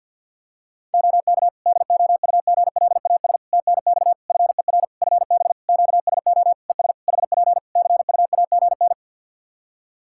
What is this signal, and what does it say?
Morse code, heard as a keyed tone, about 52 wpm, 700 Hz. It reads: OK DOUGLAS TNX FER FB QSO ES HP CUAGN
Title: Practice QSO - Exchange 4 - Ending - 50wpm